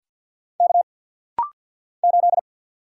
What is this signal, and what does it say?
Morse code, heard as a keyed tone, about 50 wpm, 690 Hz. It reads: K 8
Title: Single Letter-Number Alt - Rapid-Fire 50wpm